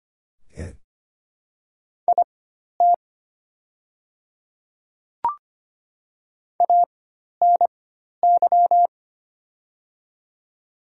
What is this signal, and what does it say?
Morse code, heard as a keyed tone, about 25 wpm, spaced as for 12 wpm, 700 Hz. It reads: IT ANY
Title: 4x Character Spacing - Top 300 Words - 25wpm